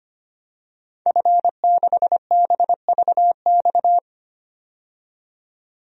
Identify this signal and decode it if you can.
Morse code, heard as a keyed tone, about 25 wpm, 700 Hz. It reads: F6BVX
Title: DXCC-100 Call Signs - 5 Characters - 25wpm